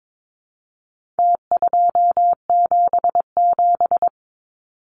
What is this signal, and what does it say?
Morse code, heard as a keyed tone, about 22 wpm, 700 Hz. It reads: T277
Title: Numbers - Contest Style - 4-Digit 22wpm